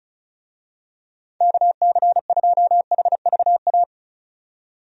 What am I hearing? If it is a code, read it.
Morse code, heard as a keyed tone, about 34 wpm, 700 Hz. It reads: KC2HVA
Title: USA Call Signs - 6 Characters - 35wpm